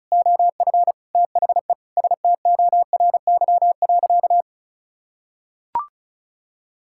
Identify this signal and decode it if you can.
Morse code, heard as a keyed tone, about 34 wpm, 700 Hz. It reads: OF THE STORY.
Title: Sentences from Top 1000 Words - Review Part 6 - 35wpm